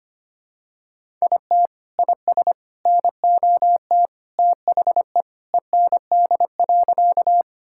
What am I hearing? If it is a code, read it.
Morse code, heard as a keyed tone, about 25 wpm, 700 Hz. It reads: IT IS NOT THE END.
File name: practice / 25wpm / Sentences with 3-Word Phrases - 25wpm